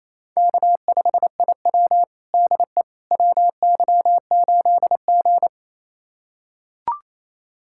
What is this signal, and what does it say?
Morse code, heard as a keyed tone, about 28 wpm, 700 Hz. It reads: K5IW DE WY8G
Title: USA Callsign DE Callsign - 4 Characters - 28wpm